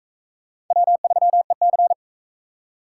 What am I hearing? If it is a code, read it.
Morse code, heard as a keyed tone, about 39 wpm, 700 Hz. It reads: W3EC